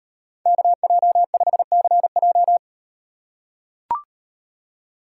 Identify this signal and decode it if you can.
Morse code, heard as a keyed tone, about 38 wpm, 700 Hz. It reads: KJ5CJ